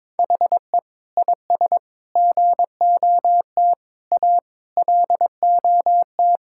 Morse code, heard as a keyed tone, about 22 wpm, 700 Hz. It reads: HE IS GOT A LOT
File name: practice / 22wpm / Sentences with 4-Word Phrases - 22wpm